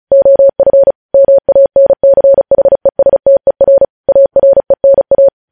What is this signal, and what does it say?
Morse code, heard as a keyed tone, about 36 wpm, 550 Hz. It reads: OF MANCHESTER ARENA
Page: News Headlines in Morse Code at 35 WPM